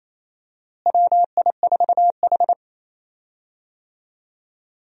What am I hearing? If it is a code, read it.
Morse code, heard as a keyed tone, about 28 wpm, 700 Hz. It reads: WI4H
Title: USA Call Signs - 4 Characters - Repeated 2x - 28wpm